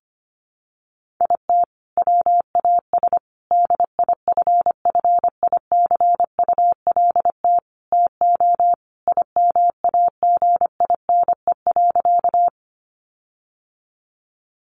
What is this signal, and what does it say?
Morse code, heard as a keyed tone, about 25 wpm, 700 Hz. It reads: IT WAS DIFFICULT TO IMAGINE.
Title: Sentences from Top 1400 Words 25wpm